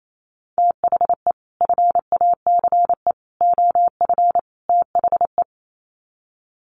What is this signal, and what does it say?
Morse code, heard as a keyed tone, about 28 wpm, 700 Hz. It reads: THE FACE OF THE
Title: Sets of 4 Words 28wpm